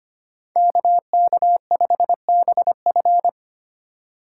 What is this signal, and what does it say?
Morse code, heard as a keyed tone, about 25 wpm, 700 Hz. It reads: KK5BF
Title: USA Call Signs - 5 Characters - 2x Character Spacing - 25wpm